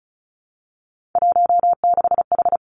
Morse code, heard as a keyed tone, about 36 wpm, 700 Hz. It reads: 16H
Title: Three Challenging Letter-Number Combinations 35wpm